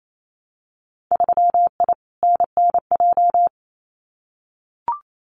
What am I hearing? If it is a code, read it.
Morse code, heard as a keyed tone, about 28 wpm, 700 Hz. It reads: 3I NNJ
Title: Winter Field Day Exchanges 28wpm